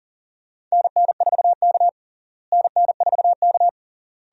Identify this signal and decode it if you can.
Morse code, heard as a keyed tone, about 40 wpm, 700 Hz. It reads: NN4K NN4K
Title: USA Call Signs - 4 Characters - Repeated 3x - 40wpm